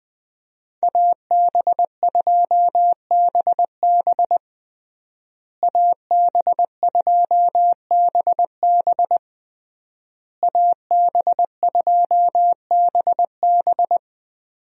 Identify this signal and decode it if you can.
Morse code, heard as a keyed tone, about 21 wpm, 700 Hz. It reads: AB2BB AB2BB AB2BB
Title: USA Call Signs - 5 Characters - Repeated 3x - 20wpm